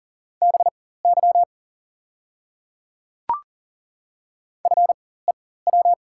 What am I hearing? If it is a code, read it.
Morse code, heard as a keyed tone, about 40 wpm, spaced as for 19 wpm, 690 Hz. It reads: BY FEW
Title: 4x Character Spacing - Top 300 Words - 40wpm